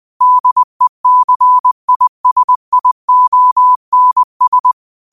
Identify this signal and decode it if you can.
Morse code, heard as a keyed tone, about 20 wpm, 1000 Hz. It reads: DECISIONS